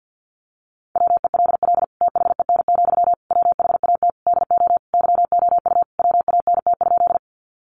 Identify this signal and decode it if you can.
Morse code, heard as a keyed tone, about 50 wpm, 700 Hz. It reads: WELL THEN, WHAT DO YOU WANT?
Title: Sentences from Top 200 Words 50wpm